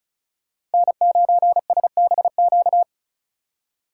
Morse code, heard as a keyed tone, about 34 wpm, 700 Hz. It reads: N9SBQ
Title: USA Call Signs - 6x Character Spacing - 35wpm